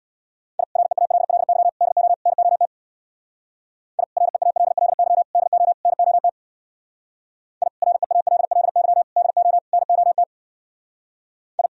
Morse code, heard as a keyed tone, about 64 wpm, 700 Hz. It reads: I REALLY DO NOT I REALLY DO NOT I REALLY DO NOT I